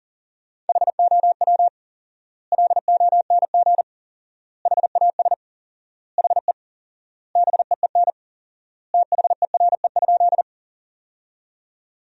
Morse code, heard as a keyed tone, about 40 wpm, 700 Hz. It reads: HOW LONG HAS HE BEEN THERE?
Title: 4x Word-Spacing - Sentences from Top 100 Words - 40wpm